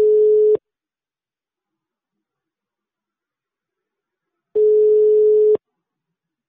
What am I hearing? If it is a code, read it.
Ringback tone (Europe, 425 Hz).
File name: sauvegarde / ACRRecordings- -jusquau 17-juin-2017 / Km€ -30€ Patrik Taxizara 2 2017 06 17 19 02 00